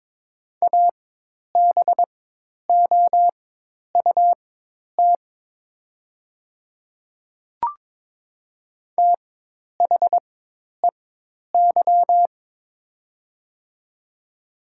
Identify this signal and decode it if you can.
Morse code, heard as a keyed tone, about 22 wpm, spaced as for 10 wpm, 700 Hz. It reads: ABOUT THEY
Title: 4x Character Spacing - Top 200 Words - 22wpm